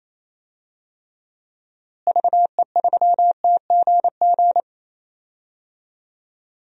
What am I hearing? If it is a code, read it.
Morse code, heard as a keyed tone, about 28 wpm, 700 Hz. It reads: VE3TGG
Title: DXCC-100 Call Signs - 6 Characters - Repeated 3x - 28wpm